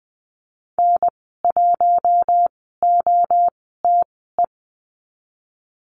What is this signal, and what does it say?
Morse code, heard as a keyed tone, about 20 wpm, spaced as for 15 wpm, 700 Hz. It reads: N1OTE
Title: 042 Lesson 8g - Number 1 - Call Signs 20wpm